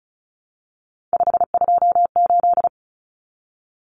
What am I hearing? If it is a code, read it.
Morse code, heard as a keyed tone, about 34 wpm, 700 Hz. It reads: 528
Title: Numbers - 3-Digit 35wpm